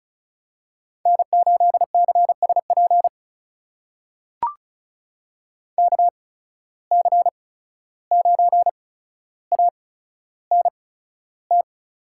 Morse code, heard as a keyed tone, about 36 wpm, spaced as for 14 wpm, 700 Hz. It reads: N8CSP KC9ANT
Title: USA Call Signs - 8x Character Spacing - 35wpm